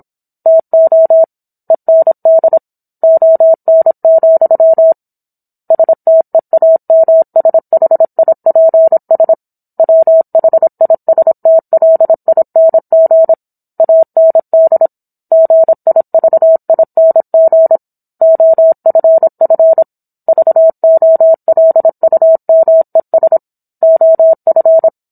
Morse code, heard as a keyed tone, about 27 wpm, 650 Hz. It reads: TO END ON, STEAMSHIPS WHISTLING AND GIVING OFF VOLUMES OF